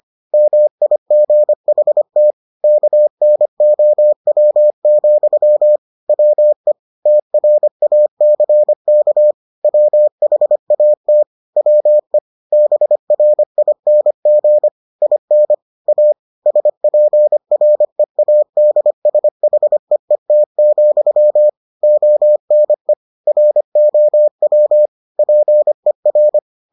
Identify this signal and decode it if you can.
Morse code, heard as a keyed tone, about 25 wpm, 600 Hz. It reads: MIGHT KNOW, WE TRACK WHAT WE BRING IN A SPREADSHEET, ONE ROW PER